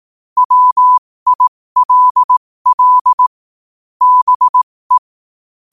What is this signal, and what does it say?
Morse code, heard as a keyed tone, about 18 wpm, spaced as for 15 wpm, 1000 Hz. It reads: WILL BE
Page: New quotes every day in morse code at 15 Words per minute.